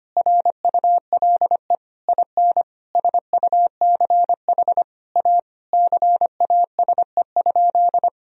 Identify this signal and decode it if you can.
Morse code, heard as a keyed tone, about 25 wpm, 700 Hz. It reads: RULE IN SUCH A CASE?
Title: Sentences from Top 1000 Words - Review Part 2 - 25wpm